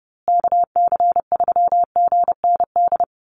Morse code, heard as a keyed tone, about 30 wpm, 700 Hz. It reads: KC3GND
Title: USA Call Signs Repeated 2x 30wpm